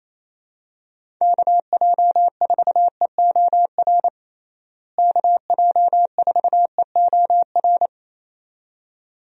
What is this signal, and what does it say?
Morse code, heard as a keyed tone, about 28 wpm, 700 Hz. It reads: KJ4EOR KJ4EOR
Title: USA Call Signs - 6 Characters - Repeated 2x - 28wpm